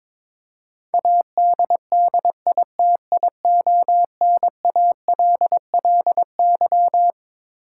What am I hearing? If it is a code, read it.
Morse code, heard as a keyed tone, about 22 wpm, 700 Hz. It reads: ADDITIONALLY
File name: Twelve Letter Words - 22wpm